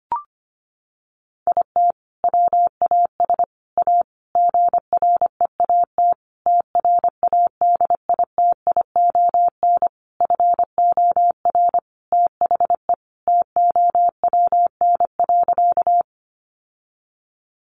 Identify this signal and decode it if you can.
Morse code, heard as a keyed tone, about 25 wpm, 700 Hz. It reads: IT WAS A GREAT TRADITION FOR THE TOWN.
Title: Sentences from Top 1500 Words 25wpm